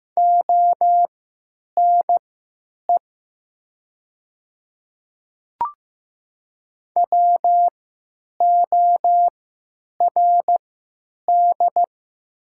Morse code, heard as a keyed tone, about 15 wpm, spaced as for 9 wpm, 700 Hz. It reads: ONE WORD
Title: 3x Character Spacing - Top 100 Words - 15wpm